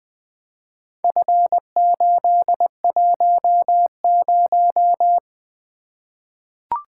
Morse code, heard as a keyed tone, about 20 wpm, 700 Hz. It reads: F810
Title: Four Letter-Number Combinations - 20wpm